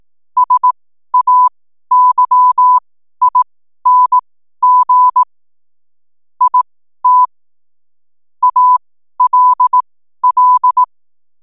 Morse code, heard as a keyed tone, about 18 wpm, spaced as for 12 wpm, 1000 Hz. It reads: SAYING IT ALL
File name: n7net / 12wpm